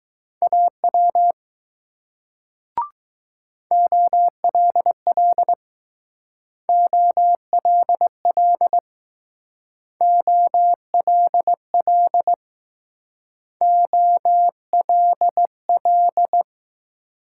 Morse code, heard as a keyed tone, about 20 wpm, 700 Hz. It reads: AW OLL OLL OLL OLL